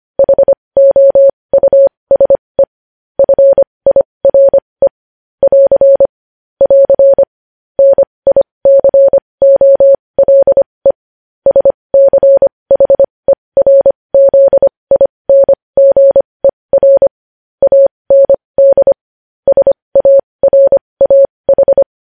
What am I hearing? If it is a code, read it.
Morse code, heard as a keyed tone, about 25 wpm, spaced as for 20 wpm, 550 Hz. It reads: HOUSE FIRE <AR> <AR> NICOLE SCHERZINGER AND SARAH
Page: news-headlines-in-morse-code-at-20-wpm